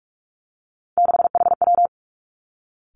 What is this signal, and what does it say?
Morse code, heard as a keyed tone, about 46 wpm, 700 Hz. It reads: 6HW
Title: Three Challenging Letter-Number Combinations 45wpm